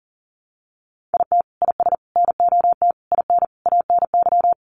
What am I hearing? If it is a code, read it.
Morse code, heard as a keyed tone, about 40 wpm, 700 Hz. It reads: IT IS NOT IN ANY
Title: Sentences from Top 1200 Words 40wpm